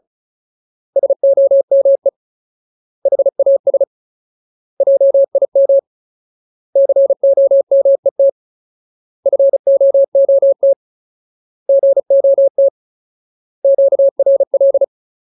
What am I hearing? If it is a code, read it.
Morse code, heard as a keyed tone, about 34 wpm, 550 Hz. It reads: SOME HAS JIM COMET FOOT GOT QRL
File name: Ultimate Word List / 35WPM Ultimate